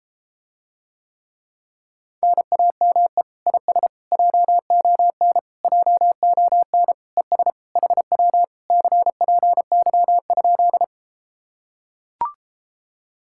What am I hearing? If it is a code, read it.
Morse code, heard as a keyed tone, about 33 wpm, 700 Hz. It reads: NAME IS JON JON ES HW CPY?